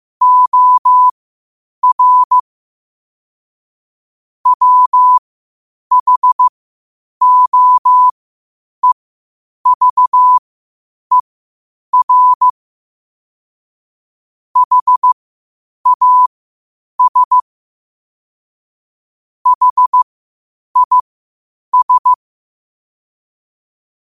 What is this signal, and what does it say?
Morse code, heard as a keyed tone, about 15 wpm, spaced as for 8 wpm, 1000 Hz. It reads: OR WHOEVER HAS HIS
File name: qotdfiles / 8wpm